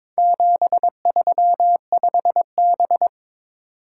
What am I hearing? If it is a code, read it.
Morse code, heard as a keyed tone, about 21 wpm, 700 Hz. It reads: 735B
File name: practice / 22wpm / Four Letter-Number Combinations - 22wpm